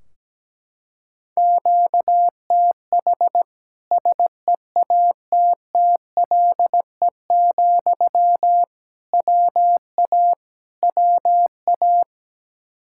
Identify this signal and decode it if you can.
Morse code, heard as a keyed tone, about 17 wpm, 700 Hz. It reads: QTH SEATTLE, WA WA
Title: Practice QSO - Exchange 1 - Beginning - 17wpm